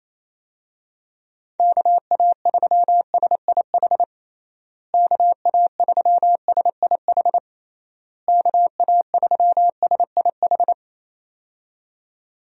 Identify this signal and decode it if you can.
Morse code, heard as a keyed tone, about 28 wpm, 700 Hz. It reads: KA3SIH KA3SIH KA3SIH
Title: USA Call Signs - 6 Characters - Repeated 3x - 28wpm